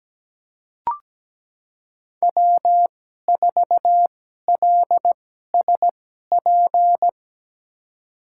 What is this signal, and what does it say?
Morse code, heard as a keyed tone, about 17 wpm, spaced as for 12 wpm, 700 Hz. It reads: W4LSP
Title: USA Call Signs - 2x Character Spacing - 17wpm